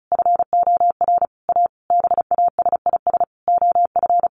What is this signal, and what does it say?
Morse code, heard as a keyed tone, about 34 wpm, 700 Hz. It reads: FOR A BASIS OF
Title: Sentences from Top 1500 Words 35wpm